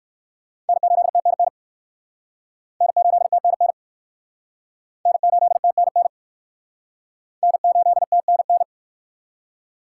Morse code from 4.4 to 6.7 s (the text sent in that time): N8TNN